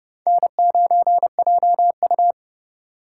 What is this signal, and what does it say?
Morse code, heard as a keyed tone, about 28 wpm, 700 Hz. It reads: N9JU